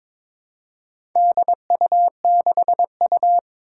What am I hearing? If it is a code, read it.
Morse code, heard as a keyed tone, about 22 wpm, 700 Hz. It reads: DU6U
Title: Four Challenging Letter-Number Combinations 22wpm